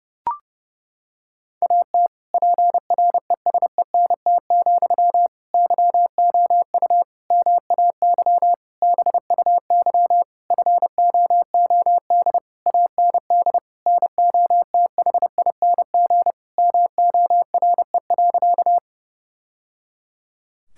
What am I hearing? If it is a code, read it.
Morse code, heard as a keyed tone, about 31 wpm, 700 Hz. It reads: AT PRESENT, YOU MAY BUY FOOD AND NOTHING MORE.
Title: Sentences from Top 1000 Words - Review Part 5 - 30wpm